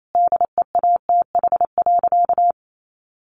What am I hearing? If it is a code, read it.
Morse code, heard as a keyed tone, about 28 wpm, 700 Hz. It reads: DEATH.
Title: Sentences from Top 1100 Words 28wpm